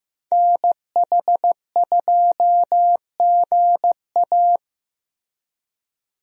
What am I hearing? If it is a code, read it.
Morse code, heard as a keyed tone, about 15 wpm, 700 Hz. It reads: NH2GA
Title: USA Call Signs - 5 Characters - Repeated 3x - 15wpm